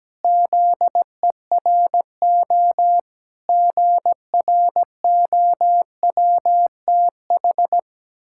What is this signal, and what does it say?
Morse code, heard as a keyed tone, about 17 wpm, 700 Hz. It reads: ZERO GROWTH
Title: Pangram Challenge v2 - 17wpm